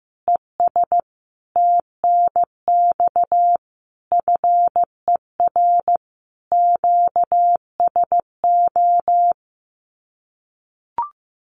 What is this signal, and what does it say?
Morse code, heard as a keyed tone, about 15 wpm, 700 Hz. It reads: ES TNX FER QSO
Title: Common QSO Phrases 15wpm